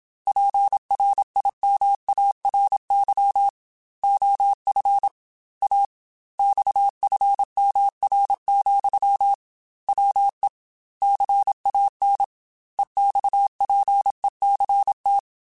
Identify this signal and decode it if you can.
Morse code, heard as a keyed tone, about 27 wpm, 800 Hz. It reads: PRIMARY OF A XFMR, WE CAN EXPECT
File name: Technical Subjects / Technical Subjects-mp3s / Sw Pwr Sup @25